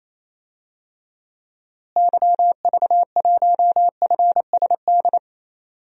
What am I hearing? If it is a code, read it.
Morse code, heard as a keyed tone, about 27 wpm, 700 Hz. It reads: YV1FSD
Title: DXCC-100 Call Signs - 6 Characters - Repeated 2x - 28wpm